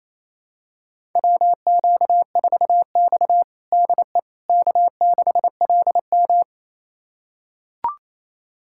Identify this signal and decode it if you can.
Morse code, heard as a keyed tone, about 28 wpm, 700 Hz. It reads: WQ4X DE K6LM
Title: USA Callsign DE Callsign - 4 Characters - 28wpm